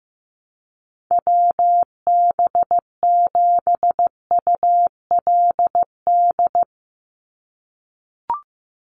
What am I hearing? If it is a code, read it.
Morse code, heard as a keyed tone, about 15 wpm, 700 Hz. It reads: WB7ULD
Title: NA SOTA Chaser Callsigns 15wpm